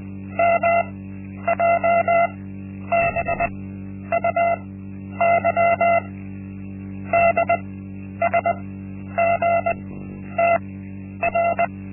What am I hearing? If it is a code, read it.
Morse code, heard as a keyed tone, about 20 wpm, spaced as for 10 wpm, 680 Hz. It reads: MJBUY DSGTR
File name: Linksmakalnis-1